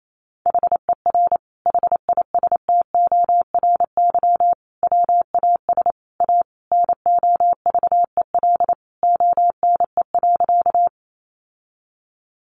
Morse code, heard as a keyed tone, about 28 wpm, 700 Hz. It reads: HER HISTORY WAS A NOVEL ONE.
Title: Sentences from Top 1300 Words 28wpm